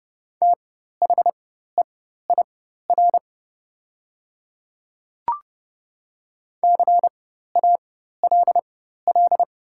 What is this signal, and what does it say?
Morse code, heard as a keyed tone, about 32 wpm, spaced as for 14 wpm, 700 Hz. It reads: THEIR CALL
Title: 4x Character Spacing - Top 300 Words - 30wpm